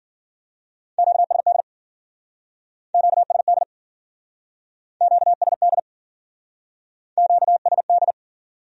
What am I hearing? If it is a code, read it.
Morse code, heard as a keyed tone, about 49 wpm, 700 Hz. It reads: QSD QSD QSD QSD
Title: Speed-Racing Q-Codes - 40wpm